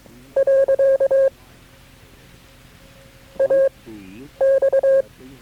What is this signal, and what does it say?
Morse code, heard as a keyed tone, about 22 wpm, spaced as for 9 wpm, 540 Hz. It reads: . AX
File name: KB6MT-tape3-side1